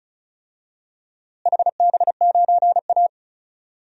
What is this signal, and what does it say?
Morse code, heard as a keyed tone, about 36 wpm, 700 Hz. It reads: HB9A
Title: DXCC-100 Call Signs - 4 Characters - Repeated 2x - 35wpm